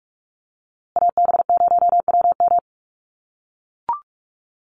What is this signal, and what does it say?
Morse code, heard as a keyed tone, about 44 wpm, 700 Hz. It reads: AB0WM
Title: USA Call Signs Repeated 2x 45wpm